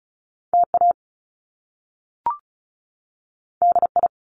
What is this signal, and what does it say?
Morse code, heard as a keyed tone, about 36 wpm, 700 Hz. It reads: TA DI
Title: Top Two Letter Combinations in Words 35wpm